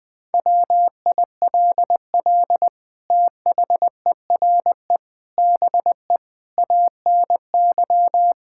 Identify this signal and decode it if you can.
Morse code, heard as a keyed tone, about 21 wpm, 700 Hz. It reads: WILL THERE BE ANY
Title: Sentences from Top 1600 Words 20wpm